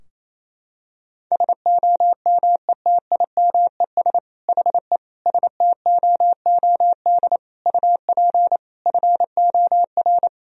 Morse code, heard as a keyed tone, about 28 wpm, 700 Hz. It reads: SOMETIMES HE STOOD UP FOR
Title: Sentences from Top 1000 Words - Review Part 4 - 28wpm